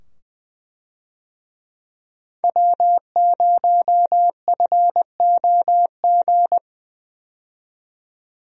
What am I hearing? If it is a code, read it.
Morse code, heard as a keyed tone, about 20 wpm, 700 Hz. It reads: W0FOG
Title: USA Call Signs - 5 Characters - 20wpm